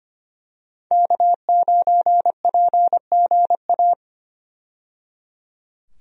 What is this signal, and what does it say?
Morse code, heard as a keyed tone, about 25 wpm, 700 Hz. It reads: K9PGA
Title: USA Call Signs - 5 Characters - 25wpm